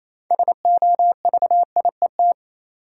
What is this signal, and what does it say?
Morse code, heard as a keyed tone, about 27 wpm, 700 Hz. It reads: SOVIET
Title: Sets of 5 Words Encore - New Word Drill - 28wpm